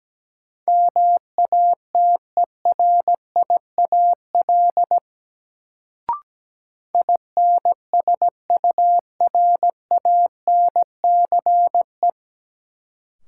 Morse code, heard as a keyed tone, about 17 wpm, 700 Hz. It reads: MATERIAL INSURANCE
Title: Sentences from Top 1100 Words - New Word Drill Alt - 17wpm